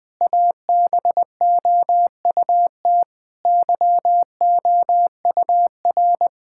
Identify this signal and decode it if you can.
Morse code, heard as a keyed tone, about 20 wpm, 700 Hz. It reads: ABOUT YOUR
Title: Sentences with 3-Word Phrases - Spoken Beginning - 20wpm